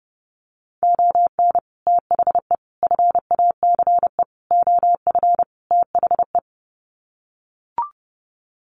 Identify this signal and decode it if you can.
Morse code, heard as a keyed tone, about 30 wpm, 700 Hz. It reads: ON THE FACE OF THE
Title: Sets of 5 Words Encore2 30wpm